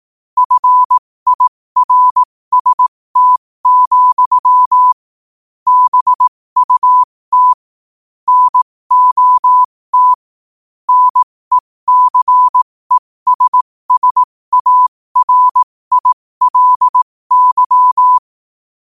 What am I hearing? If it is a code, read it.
Morse code, heard as a keyed tone, about 18 wpm, spaced as for 15 wpm, 1000 Hz. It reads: FIRST, BUT NOT NECESSARILY
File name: qotdfiles / 15wpm